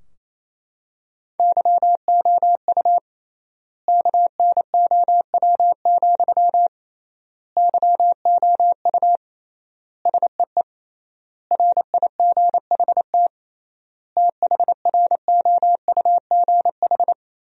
Morse code, heard as a keyed tone, about 27 wpm, 700 Hz. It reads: YOU KNOW, YOU SEE RIGHT THROUGH
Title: 3x Word-Spacing - Sentences from Top 200 Words - 28wpm